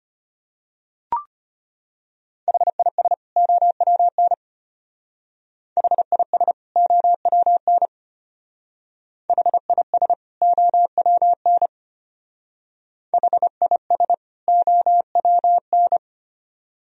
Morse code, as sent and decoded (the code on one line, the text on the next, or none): .... .. ... / --- .-- -. / .... .. ... / --- .-- -. / .... .. ... / --- .-- -. / .... .. ... / --- .-- -.
HIS OWN HIS OWN HIS OWN HIS OWN